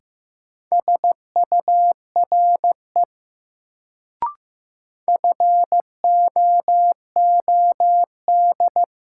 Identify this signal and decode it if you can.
Morse code, heard as a keyed tone, about 15 wpm, 700 Hz. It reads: SURE FOOD
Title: Sentences from Top 300 Words - New Word Drill - 15wpm